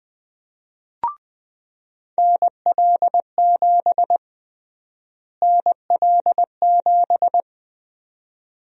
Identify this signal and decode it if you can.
Morse code, heard as a keyed tone, about 20 wpm, 700 Hz. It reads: NL7 NL7
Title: USA Call Sign Prefixes - Repeated 2x - 20wpm